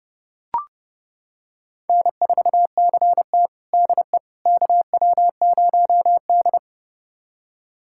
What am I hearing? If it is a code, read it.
Morse code, heard as a keyed tone, about 31 wpm, 700 Hz. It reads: N4CT DE KW0D
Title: USA Callsign DE Callsign - 4 Characters - 30wpm